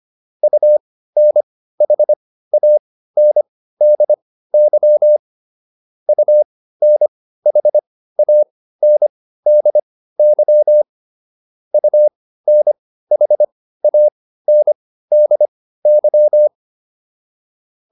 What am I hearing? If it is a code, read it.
Morse code, heard as a keyed tone, about 25 wpm, spaced as for 15 wpm, 600 Hz. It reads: UNHANDY UNHANDY UNHANDY